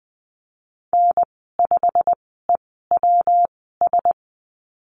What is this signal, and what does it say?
Morse code, heard as a keyed tone, about 20 wpm, spaced as for 15 wpm, 700 Hz. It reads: N5EWS
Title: 179 Lesson 29g - Slash - Call Signs 20wpm